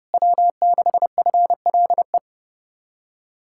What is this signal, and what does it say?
Morse code, heard as a keyed tone, about 40 wpm, 700 Hz. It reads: W6FLE